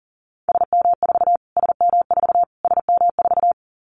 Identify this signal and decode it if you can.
Morse code, heard as a keyed tone, about 40 wpm, 700 Hz. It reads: SM4 SM4 SM4
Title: Callsign Prefixes for Top 25 DXCC - Repeated 3x - 40wpm